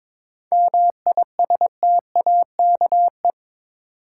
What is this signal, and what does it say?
Morse code, heard as a keyed tone, about 22 wpm, 700 Hz. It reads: MISTAKE
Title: Seven Letter Words - 22wpm